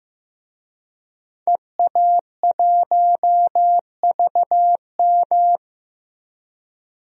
Morse code, heard as a keyed tone, about 15 wpm, 700 Hz. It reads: EA1VM
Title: DXCC-100 Call Signs - 5 Characters - 15wpm